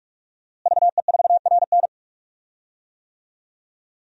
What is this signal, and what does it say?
Morse code, heard as a keyed tone, about 46 wpm, 700 Hz. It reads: VE4RN
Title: DXCC-100 Call Signs - 5 Characters - Repeated 3x - 45wpm